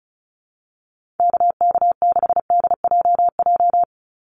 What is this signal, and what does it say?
Morse code, heard as a keyed tone, about 34 wpm, 700 Hz. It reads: KK6DJJ
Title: USA Call Signs 35wpm